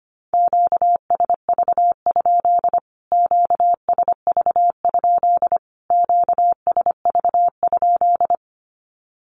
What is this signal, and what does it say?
Morse code, heard as a keyed tone, about 25 wpm, 700 Hz. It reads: QSV? QSV? QSV?
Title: Q-Codes Repeated 3x 25wpm